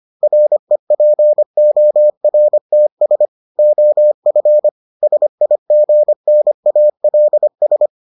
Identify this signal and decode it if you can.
Morse code, heard as a keyed tone, about 25 wpm, 600 Hz. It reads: REPORTS OF SIGNALS